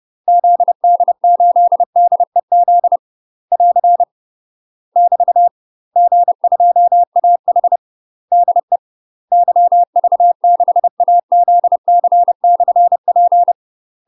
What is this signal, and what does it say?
Morse code, heard as a keyed tone, about 30 wpm, 700 Hz. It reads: ZD8DEZ <AR> <BT> G2AH DE YV6AZC/P
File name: UK MASTER QSOs TRAINING / 30WPM / UK MASTER QSOs 0022